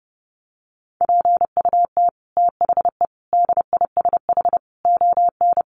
Morse code, heard as a keyed tone, about 28 wpm, 700 Hz. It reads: PUT THE DISH ON
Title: Sentences from Top 1300 Words 30wpm